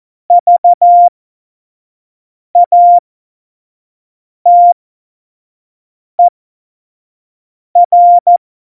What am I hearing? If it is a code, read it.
Morse code, heard as a keyed tone, about 14 wpm, spaced as for 5 wpm, 700 Hz. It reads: VATER
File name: Morsekurs / Witze / Witz 04 August 2020, 25 F70